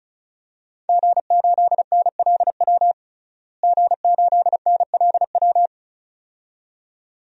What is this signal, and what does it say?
Morse code, heard as a keyed tone, about 36 wpm, 700 Hz. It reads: G8NLW G8NLW
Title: DXCC-100 Call Signs - 5 Characters - Repeated 2x - 35wpm